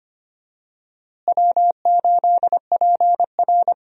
Morse code, heard as a keyed tone, about 25 wpm, 700 Hz. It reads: W8PR
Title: USA Call Signs - 4 Characters - Repeated 2x - 25wpm